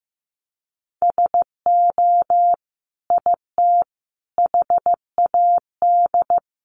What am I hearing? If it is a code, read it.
Morse code, heard as a keyed tone, about 15 wpm, 700 Hz. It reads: SO IT HAD
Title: Sentences from Top 1200 Words 15wpm